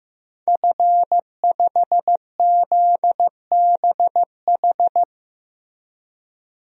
Morse code, heard as a keyed tone, about 15 wpm, 700 Hz. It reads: F5ZBH
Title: DXCC-100 Call Signs - 5 Characters - Repeated 2x - 15wpm